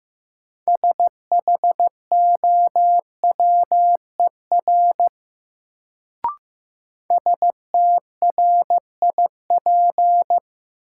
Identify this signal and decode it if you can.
Morse code, heard as a keyed tone, about 15 wpm, 700 Hz. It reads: SHOWER STRIP
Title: Sentences from Top 1600 Words - New Word Drill Alt - 15wpm